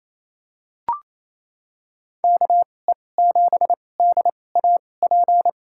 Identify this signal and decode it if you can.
Morse code, heard as a keyed tone, about 28 wpm, spaced as for 20 wpm, 700 Hz. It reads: KE7DAP
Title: USA Call Signs - 6 Characters - 2x Character Spacing - 28wpm